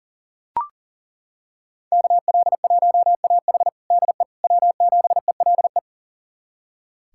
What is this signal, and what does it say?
Morse code, heard as a keyed tone, about 40 wpm, 700 Hz. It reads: KL1AH DE W7ELE
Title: USA Callsign DE Callsign 40wpm